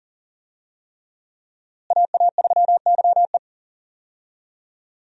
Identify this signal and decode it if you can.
Morse code, heard as a keyed tone, about 40 wpm, 700 Hz. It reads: AA3YE
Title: USA Call Signs - 5 Characters - Repeated 3x - 40wpm